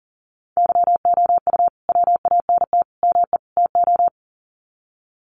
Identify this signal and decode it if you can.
Morse code, heard as a keyed tone, about 40 wpm, 700 Hz. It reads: YOU WANT ME TO
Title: Sets of 4 Words 40wpm